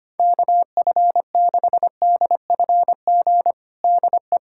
Morse code, heard as a keyed tone, about 25 wpm, 700 Hz. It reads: KF6DFG DE
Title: USA Callsign DE Callsign - 6 Characters - 25wpm